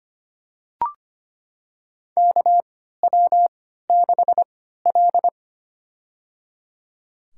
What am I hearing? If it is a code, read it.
Morse code, heard as a keyed tone, about 25 wpm, spaced as for 14 wpm, 700 Hz. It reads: KW6L
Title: USA Call Signs - 4 Characters - 3x Character Spacing - 25wpm